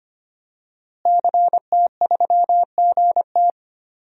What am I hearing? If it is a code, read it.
Morse code, heard as a keyed tone, about 25 wpm, 700 Hz. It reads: CT3GT
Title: DXCC-100 Call Signs - 5 Characters - Repeated 2x - 25wpm